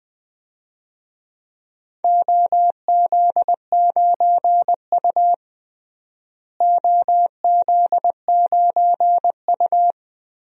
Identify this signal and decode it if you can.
Morse code, heard as a keyed tone, about 20 wpm, 700 Hz. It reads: OZ9U OZ9U
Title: DXCC-100 Call Signs - 4 Characters - Repeated 3x - 20wpm